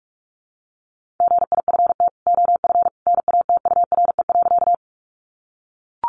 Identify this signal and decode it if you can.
Morse code, heard as a keyed tone, about 44 wpm, 700 Hz. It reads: GIFT OF NATURE.
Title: Sentences from Top 900 Words 45wpm